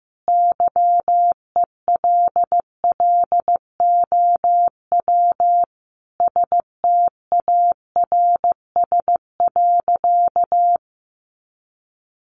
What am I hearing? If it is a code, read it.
Morse code, heard as a keyed tone, about 15 wpm, 700 Hz. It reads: YELLOW STARS.
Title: Additional Easy Phrases 15wpm